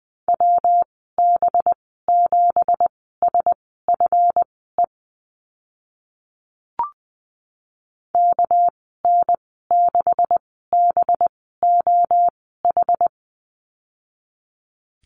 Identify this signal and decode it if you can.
Morse code, heard as a keyed tone, about 20 wpm, spaced as for 15 wpm, 700 Hz. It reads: WB7SFE KN6BOH
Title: 213 Lesson 34g - Number 8 - Call Signs 20wpm